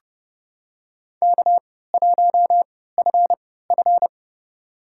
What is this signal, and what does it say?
Morse code, heard as a keyed tone, about 29 wpm, spaced as for 17 wpm, 700 Hz. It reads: K1FF
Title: USA Call Signs - 4 Characters - 3x Character Spacing - 30wpm